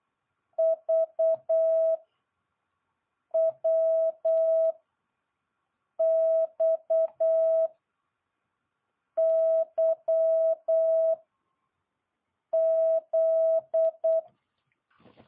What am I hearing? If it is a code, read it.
Morse code, heard as a keyed tone, about 8 wpm, 650 Hz. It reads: V W X Y Z
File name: jh8cbh